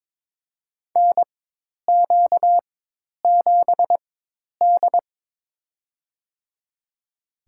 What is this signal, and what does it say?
Morse code, heard as a keyed tone, about 22 wpm, spaced as for 10 wpm, 700 Hz. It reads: NQ7D
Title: USA Call Signs - 4 Characters - 4x Character Spacing - 22wpm